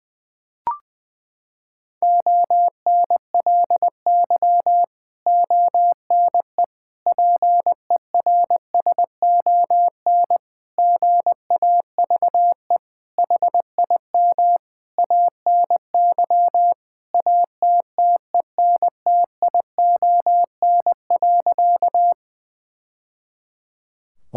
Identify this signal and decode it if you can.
Morse code, heard as a keyed tone, about 20 wpm, 700 Hz. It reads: ONLY ONE PERSON GAVE HIM ANY ATTENTION.
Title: Sentences from Top 1000 Words - Review Part 6 - 20wpm